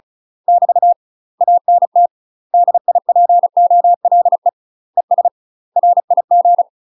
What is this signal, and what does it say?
Morse code, heard as a keyed tone, about 34 wpm, 700 Hz. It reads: <BT> ANT DIPOLE ES RIG
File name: UK MASTER QSOs TRAINING / 35WPM / UK MASTER QSOs 0004